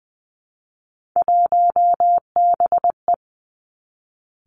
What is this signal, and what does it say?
Morse code, heard as a keyed tone, about 19 wpm, 700 Hz. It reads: 1BE
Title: Three Challenging Letter-Number Combinations 20wpm